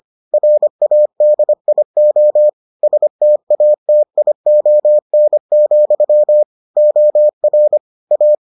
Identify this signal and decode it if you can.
Morse code, heard as a keyed tone, about 25 wpm, 600 Hz. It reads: RADIO STATION, OR A